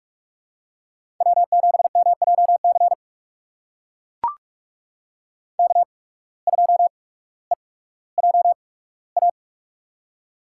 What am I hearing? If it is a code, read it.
Morse code, heard as a keyed tone, about 44 wpm, spaced as for 19 wpm, 690 Hz. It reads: W7MJC K2EJA
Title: USA Call Signs - 5 Characters - 8x Character Spacing - 45wpm